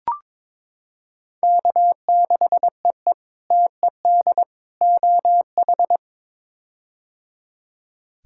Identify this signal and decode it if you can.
Morse code, heard as a keyed tone, about 22 wpm, 700 Hz. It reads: K6EE TED OH
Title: K1USN SST QSO Practice - 22wpm